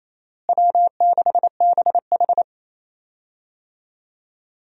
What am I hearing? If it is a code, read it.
Morse code, heard as a keyed tone, about 28 wpm, 700 Hz. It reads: W6BH
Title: USA Call Signs - 4 Characters - Repeated 2x - 28wpm